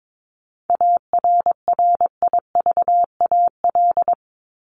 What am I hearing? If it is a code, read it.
Morse code, heard as a keyed tone, about 22 wpm, 700 Hz. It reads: ARRIVAL
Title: Top 1751-2000 Words 22wpm